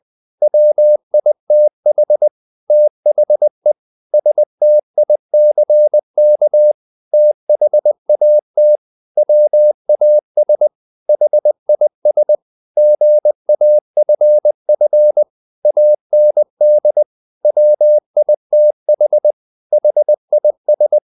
Morse code, heard as a keyed tone, about 20 wpm, 600 Hz. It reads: WITH THE STICK THAT WAS HIS GAFF AND WITH HIS
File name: the old man and the sea part 2 20wpm